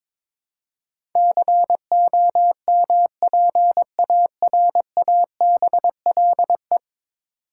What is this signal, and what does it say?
Morse code, heard as a keyed tone, about 22 wpm, 700 Hz. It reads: COMPARABLE
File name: practice / 22wpm / 10 Letter Words - First 3 Letters Given - 22wpm